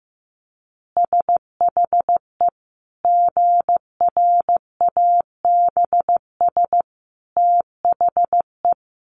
Morse code, heard as a keyed tone, about 15 wpm, 700 Hz. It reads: SHE GRABS THE
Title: Easy Phrases 15wpm